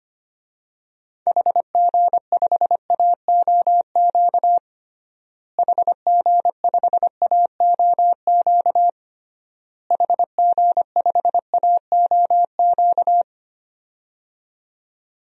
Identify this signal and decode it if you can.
Morse code, heard as a keyed tone, about 25 wpm, 700 Hz. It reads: HG5AOQ HG5AOQ HG5AOQ
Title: DXCC-100 Call Signs - 6 Characters - Repeated 3x - 25wpm